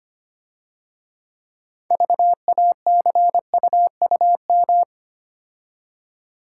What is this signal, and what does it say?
Morse code, heard as a keyed tone, about 30 wpm, 700 Hz. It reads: VACUUM
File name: Easy Words - Speed-Racing with Hints - 25wpm